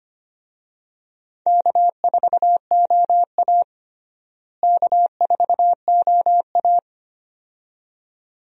Morse code, heard as a keyed tone, about 25 wpm, 700 Hz. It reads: K4OA K4OA
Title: USA Call Signs - 4 Characters - Repeated 2x - 25wpm